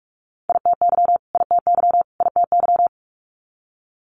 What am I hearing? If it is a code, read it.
Morse code, heard as a keyed tone, about 46 wpm, 700 Hz. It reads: ITY ITY ITY
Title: Multi-letter Phonemes Repeated 3x 45wpm